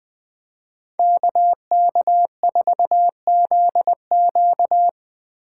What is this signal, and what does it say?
Morse code, heard as a keyed tone, about 20 wpm, 700 Hz. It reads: KK4ZQ